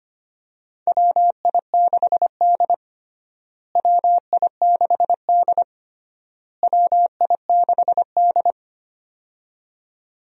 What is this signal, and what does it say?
Morse code, heard as a keyed tone, about 25 wpm, 700 Hz. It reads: WI6D WI6D WI6D
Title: USA Call Signs - 4 Characters - Repeated 3x - 25wpm